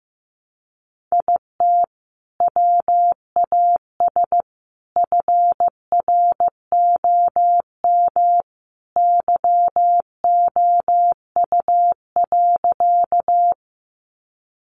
Morse code, heard as a keyed tone, about 14 wpm, 700 Hz. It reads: IT WAS FROM YOU.
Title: Sentences from Top 100 Words 15wpm